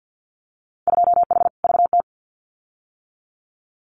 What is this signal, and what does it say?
Morse code, heard as a keyed tone, about 50 wpm, 700 Hz. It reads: 2H VT
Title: Winter Field Day Exchanges 50wpm